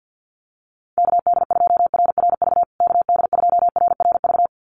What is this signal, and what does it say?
Morse code, heard as a keyed tone, about 50 wpm, 700 Hz. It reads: KD2RRV KD2RRV
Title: USA Call Signs Repeated 2x 50wpm